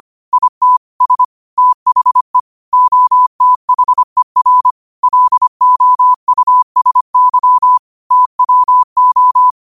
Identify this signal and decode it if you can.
Morse code, heard as a keyed tone, about 25 wpm, 1000 Hz. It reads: IT S THE OTHER LOUSY TWO
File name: qotdfiles / 25wpm